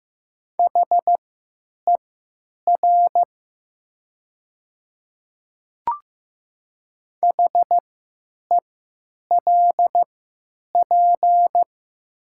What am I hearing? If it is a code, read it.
Morse code, heard as a keyed tone, about 15 wpm, spaced as for 9 wpm, 700 Hz. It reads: HER HELP
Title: 3x Character Spacing - Top 500 Words - 15wpm